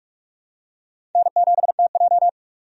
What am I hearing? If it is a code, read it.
Morse code, heard as a keyed tone, about 46 wpm, 700 Hz. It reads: N7TJ
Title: USA Call Signs - 4 Characters - 45wpm